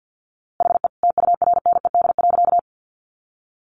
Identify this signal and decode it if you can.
Morse code, heard as a keyed tone, about 50 wpm, 700 Hz. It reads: HE TURNED.
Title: A Sound Of Thunder 50wpm